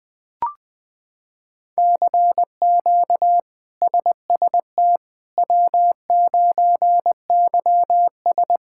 Morse code, heard as a keyed tone, about 20 wpm, 700 Hz. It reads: CQ SST W9YS
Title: K1USN SST QSO Practice - 20wpm